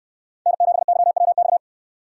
Morse code, heard as a keyed tone, about 68 wpm, 700 Hz. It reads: N7YWX